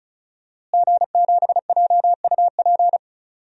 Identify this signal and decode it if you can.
Morse code, heard as a keyed tone, about 36 wpm, 700 Hz. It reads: G7JUP
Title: DXCC-100 Call Signs - 5 Characters - Repeated 2x - 35wpm